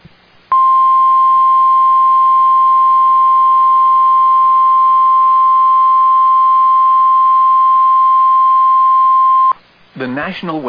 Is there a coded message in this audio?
Yes: Emergency Alert System (SAME) tones (the 1050 Hz attention tone) from 0.5 to 9.6 s, with speech.